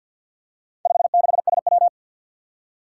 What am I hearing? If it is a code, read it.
Morse code, heard as a keyed tone, about 52 wpm, 700 Hz. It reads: 56SW
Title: Four Challenging Letter-Number Combinations 50wpm